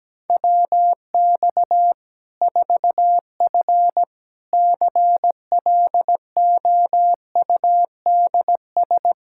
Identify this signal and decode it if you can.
Morse code, heard as a keyed tone, about 17 wpm, 700 Hz. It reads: WX 4F CLOUDS
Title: Practice QSO - Exchange 2 - Info - 17wpm